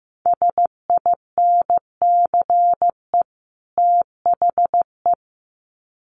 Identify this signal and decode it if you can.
Morse code, heard as a keyed tone, about 15 wpm, 700 Hz. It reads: SINCE THE
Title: Sets of 2 Words 15wpm